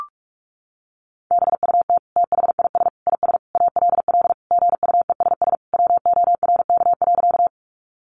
Morse code, heard as a keyed tone, about 44 wpm, 700 Hz. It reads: BUT THIS IS ALL GUESS WORK.
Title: Sentences from Top 1500 Words 45wpm